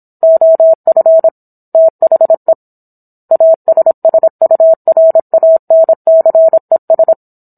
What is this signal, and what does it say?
Morse code, heard as a keyed tone, about 27 wpm, 650 Hz. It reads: OF THE ASSURANCES